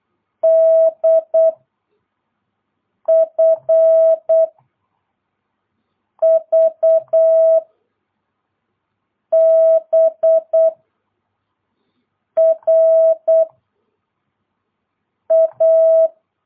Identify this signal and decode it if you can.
Morse code, heard as a keyed tone, about 8 wpm, 650 Hz. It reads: D F V B R A
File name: jh8cbh / 2022cw